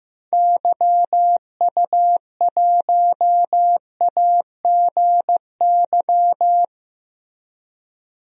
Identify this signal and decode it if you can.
Morse code, heard as a keyed tone, about 15 wpm, 700 Hz. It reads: YU1AGY
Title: DXCC-100 Call Signs - 6 Characters - 15wpm